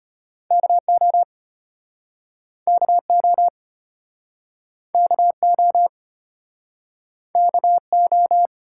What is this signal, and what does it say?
Morse code, heard as a keyed tone, about 34 wpm, 700 Hz. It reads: KO KO KO KO